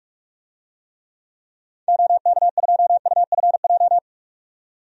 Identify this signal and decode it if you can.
Morse code, heard as a keyed tone, about 42 wpm, 700 Hz. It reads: OK2UFJ
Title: DXCC-100 Call Signs - 6 Characters - 45wpm